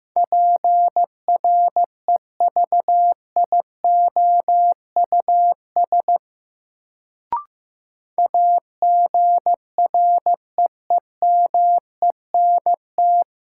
Morse code, heard as a keyed tone, about 15 wpm, 700 Hz. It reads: PREVIOUS AGREEMENT
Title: Sentences from Top 1500 Words - New Word Drill Alt - 15wpm